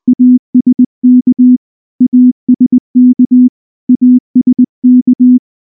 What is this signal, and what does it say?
Morse code, heard as a keyed tone, about 19 wpm, 250 Hz. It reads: ASK ASK ASK
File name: morse / morse-training-20WPM-2